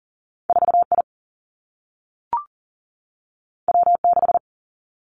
Morse code, heard as a keyed tone, about 38 wpm, 700 Hz. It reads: 4 I W 6
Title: Two Challenging Letter-Number Combinations 40wpm